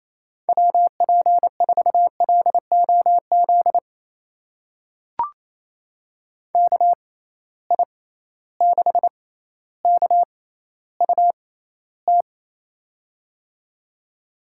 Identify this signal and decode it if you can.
Morse code, heard as a keyed tone, about 28 wpm, spaced as for 14 wpm, 700 Hz. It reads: WP4LOZ KI6KUT
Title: USA Call Signs - 6x Character Spacing - 28wpm